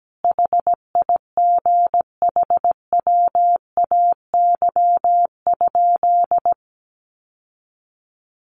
Morse code, heard as a keyed tone, about 17 wpm, 700 Hz. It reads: HIGHWAY?
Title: Sentences from Top 1400 Words 17wpm